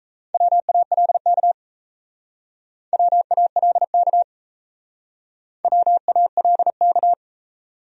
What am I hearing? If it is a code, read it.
Morse code, heard as a keyed tone, about 35 wpm, 700 Hz. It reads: WALK WALK WALK
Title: Speed-Racing Top 500 Words 28wpm